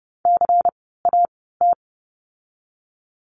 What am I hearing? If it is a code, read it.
Morse code, heard as a keyed tone, about 30 wpm, spaced as for 17 wpm, 700 Hz. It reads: CAT
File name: Lesson 16f - Letter c - Words 30wpm